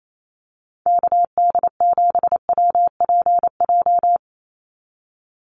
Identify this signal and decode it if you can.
Morse code, heard as a keyed tone, about 28 wpm, 700 Hz. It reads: KD7WPJ
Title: NA SOTA Activator Callsigns 28wpm